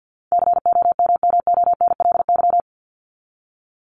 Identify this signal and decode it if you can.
Morse code, heard as a keyed tone, about 54 wpm, 700 Hz. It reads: COMMONLY
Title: LY Suffix - Top 100 Words with Base Words 50wpm